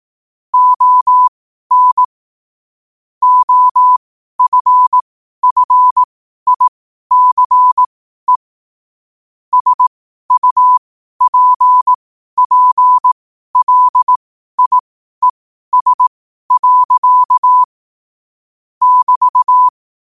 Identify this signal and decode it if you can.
Morse code, heard as a keyed tone, about 18 wpm, spaced as for 12 wpm, 1000 Hz. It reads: ON OFFICE SUPPLIES. <BT>
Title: 12 WPM morse code quotes for Fri, 01 Mar 2024